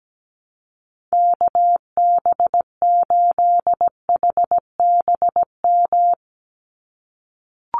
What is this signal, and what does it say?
Morse code, heard as a keyed tone, about 17 wpm, 700 Hz. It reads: KB8HBM
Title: USA Call Signs 17wpm